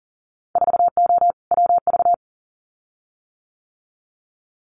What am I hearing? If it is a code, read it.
Morse code, heard as a keyed tone, about 38 wpm, 700 Hz. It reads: 4O WV
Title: Winter Field Day Exchanges Encore 40wpm